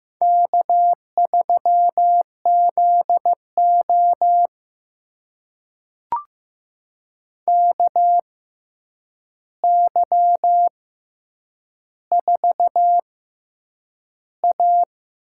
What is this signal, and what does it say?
Morse code, heard as a keyed tone, about 15 wpm, spaced as for 8 wpm, 700 Hz. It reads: K3ZO KY4A
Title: USA Call Signs - 4 Characters - 6x Character Spacing - 15wpm